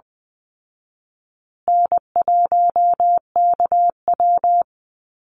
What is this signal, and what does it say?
Morse code, heard as a keyed tone, about 19 wpm, 700 Hz. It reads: N1KW
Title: Easy Call Signs Repeated 3x - 20wpm